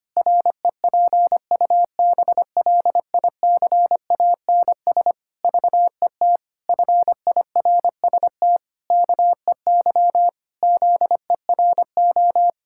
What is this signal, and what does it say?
Morse code, heard as a keyed tone, about 25 wpm, 700 Hz. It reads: REPUBLICANS VET FIRST KEY ZERO
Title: Pangram Challenge v2 - 25wpm